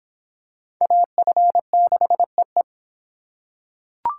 Morse code, heard as a keyed tone, about 23 wpm, 700 Hz. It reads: AF6EE